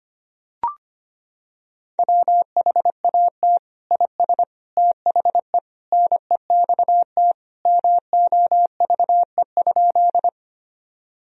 Morse code, heard as a keyed tone, about 25 wpm, 700 Hz. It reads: WHAT IS THE NEXT MOVE?
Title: Sentences from Top 1000 Words - Review Part 1 - 25wpm